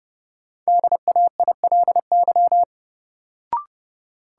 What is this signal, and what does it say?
Morse code, heard as a keyed tone, about 30 wpm, 700 Hz. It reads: DAILY
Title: Sentences from Top 1100 Words - New Word Drill - 30wpm